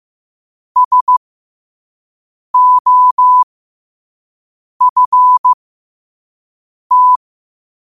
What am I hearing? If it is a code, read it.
Morse code, heard as a keyed tone, about 15 wpm, spaced as for 5 wpm, 1000 Hz. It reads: SOFT
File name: qotdfiles / 5wpm